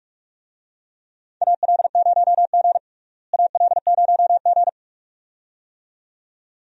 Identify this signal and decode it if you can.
Morse code, heard as a keyed tone, about 46 wpm, 700 Hz. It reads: AL0G AL0G
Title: USA Call Signs - 4 Characters - Repeated 2x - 45wpm